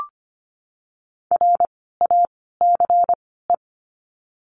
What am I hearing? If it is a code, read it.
Morse code, heard as a keyed tone, about 24 wpm, spaced as for 16 wpm, 700 Hz. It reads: RACE
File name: course / 25:10wpm / 094 Lesson 16f - Letter c - Words 25wpm